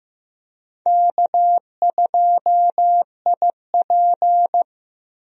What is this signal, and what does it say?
Morse code, heard as a keyed tone, about 15 wpm, 700 Hz. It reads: K2IP
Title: USA Call Signs - 4 Characters - Repeated 3x - 15wpm